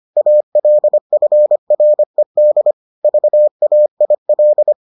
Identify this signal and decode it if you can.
Morse code, heard as a keyed tone, about 25 wpm, 600 Hz. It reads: ALFRED VAIL